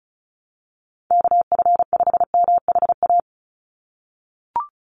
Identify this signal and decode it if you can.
Morse code, heard as a keyed tone, about 36 wpm, 700 Hz. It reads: KF5MHA
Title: USA Call Signs 35wpm